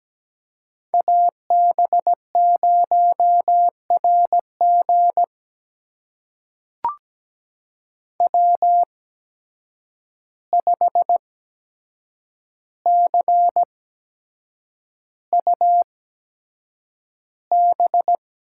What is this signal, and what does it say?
Morse code, heard as a keyed tone, about 17 wpm, spaced as for 7 wpm, 700 Hz. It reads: AB0RG W5CUB
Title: USA Call Signs - 5 Characters - 8x Character Spacing - 17wpm